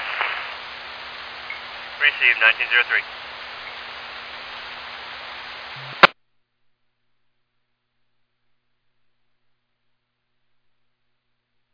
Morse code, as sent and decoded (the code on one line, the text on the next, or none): none